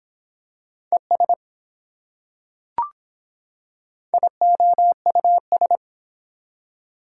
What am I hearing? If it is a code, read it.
Morse code, heard as a keyed tone, about 24 wpm, 700 Hz. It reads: ES IOUS